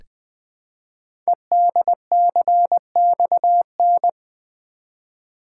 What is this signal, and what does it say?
Morse code, heard as a keyed tone, about 20 wpm, 700 Hz. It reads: EDCXN
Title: Code Groups - 5 Letters - 20wpm